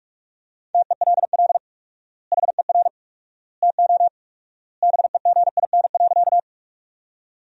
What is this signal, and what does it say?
Morse code, heard as a keyed tone, about 43 wpm, 700 Hz. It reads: TELL HER TO BEGIN.
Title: 4x Word-Spacing - Sentences from Top 300 Words - 45wpm